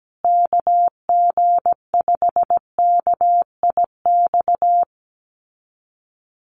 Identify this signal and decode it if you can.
Morse code, heard as a keyed tone, about 17 wpm, 700 Hz. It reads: KG5KIX
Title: USA Call Signs Repeated 2x 17wpm